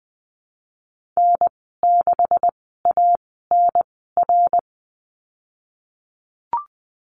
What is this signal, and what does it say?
Morse code, heard as a keyed tone, about 20 wpm, spaced as for 14 wpm, 700 Hz. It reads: N6ANR
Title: 147 Lesson 24g - Letter y - Call Signs 20wpm